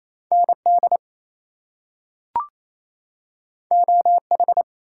Morse code, heard as a keyed tone, about 28 wpm, 700 Hz. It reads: ND OH
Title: US States and Canadian Provinces 28wpm